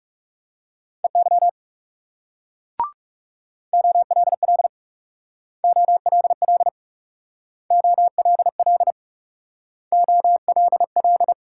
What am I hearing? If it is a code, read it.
Morse code, heard as a keyed tone, about 38 wpm, 700 Hz. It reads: EY OLL OLL OLL OLL